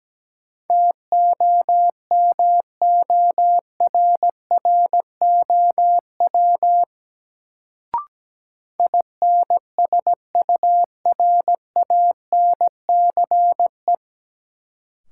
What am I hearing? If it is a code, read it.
Morse code, heard as a keyed tone, about 17 wpm, 700 Hz. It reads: TOMORROW INSURANCE
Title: Sentences from Top 1100 Words - New Word Drill - 17wpm